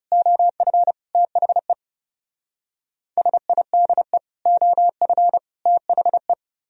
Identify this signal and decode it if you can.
Morse code, heard as a keyed tone, about 36 wpm, 700 Hz. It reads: OF THE SIDE OF THE
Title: Speed-Racing - Sets of 3 Words 30wpm